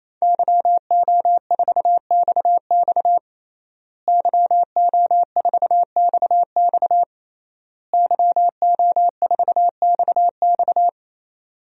Morse code, heard as a keyed tone, about 28 wpm, 700 Hz. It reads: YO4XX YO4XX YO4XX
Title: DXCC-100 Call Signs - 5 Characters - Repeated 3x - 28wpm